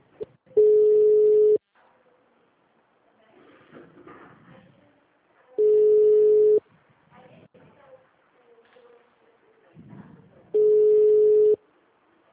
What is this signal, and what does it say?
Ringback tone (Europe, 425 Hz), with speech.